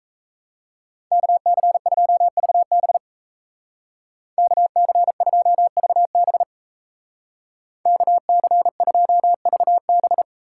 Morse code, heard as a keyed tone, about 33 wpm, 700 Hz. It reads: KC2VB KC2VB KC2VB
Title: Speed Racing - USA 2x2 Call Signs 28wpm